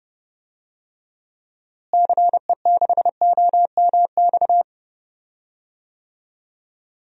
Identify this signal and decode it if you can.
Morse code, heard as a keyed tone, about 28 wpm, 700 Hz. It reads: CE6OMX
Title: DXCC-100 Call Signs - 6 Characters - Repeated 2x - 30wpm